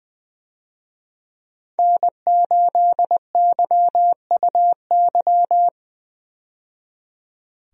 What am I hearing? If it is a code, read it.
Morse code, heard as a keyed tone, about 20 wpm, 700 Hz. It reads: N8YUY
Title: USA Call Signs - 5 Characters - 20wpm